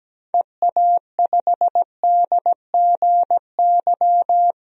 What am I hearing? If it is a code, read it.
Morse code, heard as a keyed tone, about 17 wpm, 700 Hz. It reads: EA5DGY
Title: DXCC-100 Call Signs - 6 Characters - 17wpm